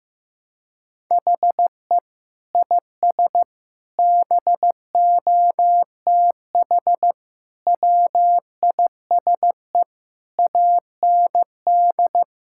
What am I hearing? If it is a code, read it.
Morse code, heard as a keyed tone, about 15 wpm, 700 Hz. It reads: HE IS BOTH WISE AND
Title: Sentences from Top 1600 Words 15wpm